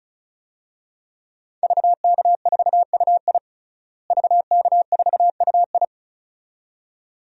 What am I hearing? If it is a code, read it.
Morse code, heard as a keyed tone, about 36 wpm, 700 Hz. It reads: VK4UI VK4UI
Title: DXCC-100 Call Signs - 5 Characters - Repeated 2x - 35wpm